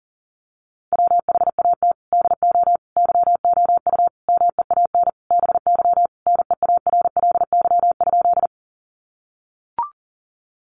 Morse code, heard as a keyed tone, about 40 wpm, 700 Hz. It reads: WHAT DO YOU MEAN BY NEARLY?
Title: Sentences from Top 1300 Words 40wpm